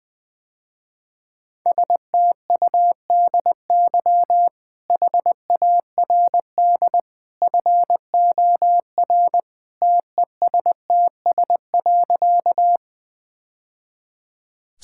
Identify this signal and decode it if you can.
Morse code, heard as a keyed tone, about 20 wpm, 700 Hz. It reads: STUDY HARD FOR TESTS.
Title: Easy Sentences with Hints - 20wpm